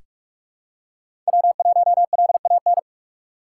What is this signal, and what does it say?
Morse code, heard as a keyed tone, about 46 wpm, 700 Hz. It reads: W1LAN
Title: USA Call Signs - 5 Characters - 2x Character Spacing - 45wpm